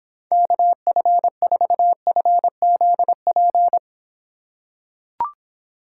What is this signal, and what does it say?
Morse code, heard as a keyed tone, about 26 wpm, 700 Hz. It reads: KF4FZP